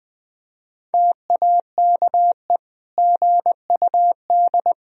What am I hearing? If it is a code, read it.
Morse code, heard as a keyed tone, about 20 wpm, 700 Hz. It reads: TAKE GUD
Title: Sentences from Top 400 Words and Abbreviations 20wpm